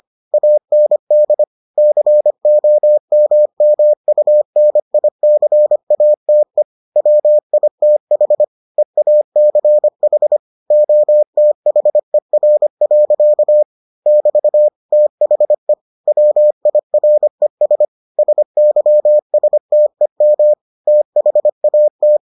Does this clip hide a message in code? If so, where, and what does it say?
Yes: Morse code from 0.3 to 22.2 s, heard as a keyed tone, about 25 wpm, 600 Hz. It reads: AND COMMUNICATE WITH EACH OTHER. <BT> THE WIRES SYSTEM THAT